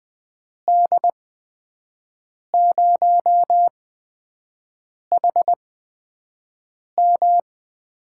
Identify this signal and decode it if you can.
Morse code, heard as a keyed tone, about 20 wpm, spaced as for 5 wpm, 700 Hz. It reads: D0HM
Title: USA Call Signs - 5 Characters - 8x Character Spacing - 20wpm